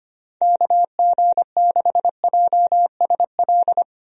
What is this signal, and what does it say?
Morse code, heard as a keyed tone, about 25 wpm, 700 Hz. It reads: KG6JSL
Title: USA Call Signs - 6 Characters - Repeated 3x - 25wpm